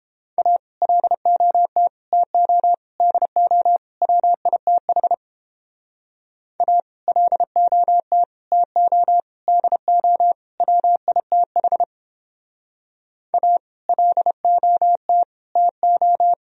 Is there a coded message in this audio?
Yes: Morse code from 0.4 to 16.3 s, heard as a keyed tone, about 29 wpm, 700 Hz. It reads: A LOT TO DO WITH A LOT TO DO WITH A LOT TO